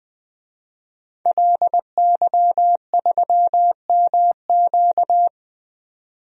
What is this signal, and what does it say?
Morse code, heard as a keyed tone, about 20 wpm, 700 Hz. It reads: LY3MQ
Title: DXCC-100 Call Signs - 5 Characters - 20wpm